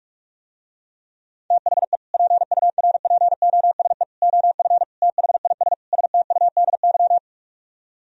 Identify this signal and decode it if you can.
Morse code, heard as a keyed tone, about 44 wpm, 700 Hz. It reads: THE PURPOSE OF THIS STUDY